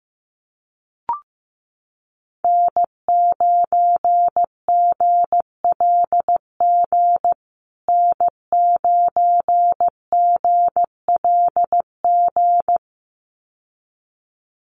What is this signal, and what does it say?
Morse code, heard as a keyed tone, about 15 wpm, 700 Hz. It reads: N9GLG N9GLG
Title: USA Call Signs Repeated 2x 15wpm